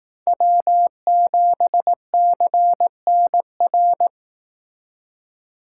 Morse code, heard as a keyed tone, about 18 wpm, 700 Hz. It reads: W7CNR
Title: Speed Racing - USA 1x3 Call Signs 15wpm